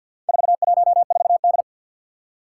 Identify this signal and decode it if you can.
Morse code, heard as a keyed tone, about 48 wpm, 700 Hz. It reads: 414D
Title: Four Challenging Letter-Number Combinations 50wpm